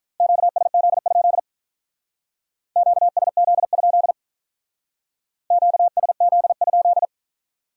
Morse code, heard as a keyed tone, about 44 wpm, 700 Hz. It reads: QSZ? QSZ? QSZ?